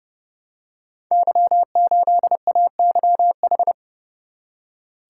Morse code, heard as a keyed tone, about 30 wpm, 700 Hz. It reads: Y8AYH
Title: Code Groups - 5 Letter:Numbers - 30wpm